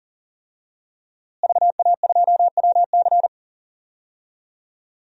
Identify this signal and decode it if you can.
Morse code, heard as a keyed tone, about 40 wpm, 700 Hz. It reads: VA2WC
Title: DXCC-100 Call Signs - 5 Characters - Repeated 3x - 40wpm